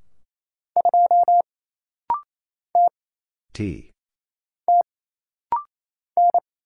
Morse code, heard as a keyed tone, about 27 wpm, 700 Hz. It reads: 2 T T N
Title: Single Letter-Number Alt - Rapid-Fire 28wpm